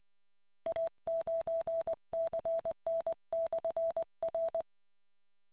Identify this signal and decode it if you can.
Morse code, heard as a keyed tone, about 20 wpm, 660 Hz. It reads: A9CN/R